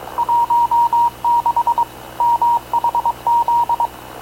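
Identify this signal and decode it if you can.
Morse code, heard as a keyed tone, about 22 wpm, 940 Hz. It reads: 16 MHZ
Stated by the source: Havana Radio / CLA sending Spanish press (PX) in Morse, recorded by Radio Officer Paul Dumesnil onboard CNAV QUEST/CZDO.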